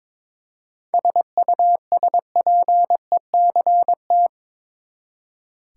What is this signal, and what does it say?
Morse code, heard as a keyed tone, about 21 wpm, 700 Hz. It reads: SUSPECT
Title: Seven Letter Words - 22wpm